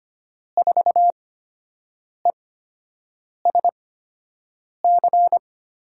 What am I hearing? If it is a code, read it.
Morse code, heard as a keyed tone, about 25 wpm, spaced as for 7 wpm, 700 Hz. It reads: 4ESC
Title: USA Call Signs - 5 Characters - 8x Character Spacing - 25wpm